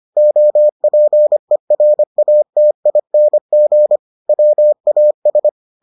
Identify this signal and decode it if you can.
Morse code, heard as a keyed tone, about 25 wpm, 600 Hz. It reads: OPERATING WAS